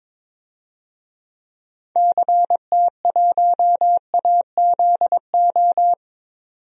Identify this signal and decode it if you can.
Morse code, heard as a keyed tone, about 22 wpm, 700 Hz. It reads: CT1AZO
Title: DXCC-100 Call Signs - 6 Characters - 22wpm